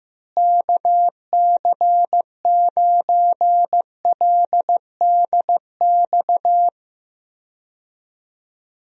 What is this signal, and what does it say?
Morse code, heard as a keyed tone, about 15 wpm, 700 Hz. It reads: KC9LDX
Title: USA Call Signs - 6 Characters - Repeated 3x - 15wpm